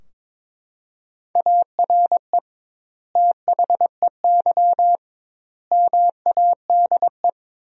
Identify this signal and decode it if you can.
Morse code, heard as a keyed tone, about 22 wpm, 700 Hz. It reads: ARE THEY MADE
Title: 2x Word-Spacing - Sentences from Top 400 Words - 22wpm